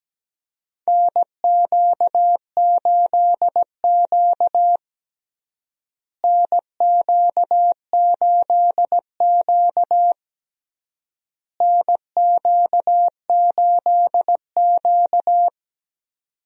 Morse code, heard as a keyed tone, about 17 wpm, 700 Hz. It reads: NQ8Q NQ8Q NQ8Q
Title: USA Call Signs - 4 Characters - Repeated 3x - 17wpm